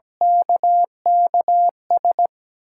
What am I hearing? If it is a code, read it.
Morse code, heard as a keyed tone, about 17 wpm, 700 Hz. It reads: KKS
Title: USA Call Signs - 6 Characters - 4x Character Spacing - 17wpm